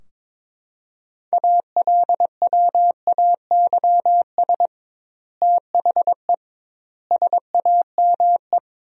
Morse code, heard as a keyed tone, about 22 wpm, 700 Hz. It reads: ALWAYS THE SAME
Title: 2x Word-Spacing - Sentences from Top 300 Words - 22wpm